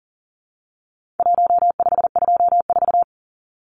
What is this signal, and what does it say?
Morse code, heard as a keyed tone, about 40 wpm, 700 Hz. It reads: 1524
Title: Numbers - 4-Digit 40wpm